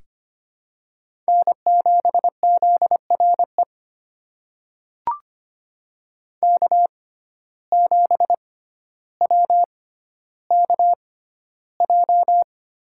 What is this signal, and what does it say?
Morse code, heard as a keyed tone, about 25 wpm, spaced as for 13 wpm, 700 Hz. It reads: N7ZRE K7WKJ
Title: USA Call Signs - 5 Characters - 6x Character Spacing - 25wpm